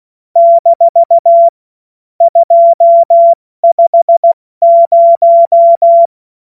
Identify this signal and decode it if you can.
Morse code, heard as a keyed tone, about 16 wpm, spaced as for 14 wpm, 680 Hz. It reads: - 250
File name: Morsekurs / Witze / Witz 31. August 2021, 70 F80